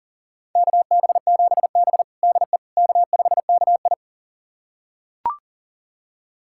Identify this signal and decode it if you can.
Morse code, heard as a keyed tone, about 40 wpm, 700 Hz. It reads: KB7B DE K5KI
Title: USA Callsign DE Callsign - 4 Characters - 40wpm